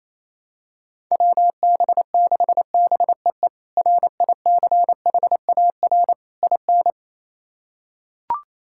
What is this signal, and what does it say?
Morse code, heard as a keyed tone, about 28 wpm, 700 Hz. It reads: WB6BEE RICHAR IN
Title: K1USN SST QSO Practice - 28wpm